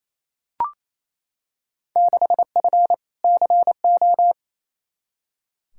Morse code, heard as a keyed tone, about 28 wpm, 700 Hz. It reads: 6F CO
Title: Field Day Exchanges Encore - 28wpm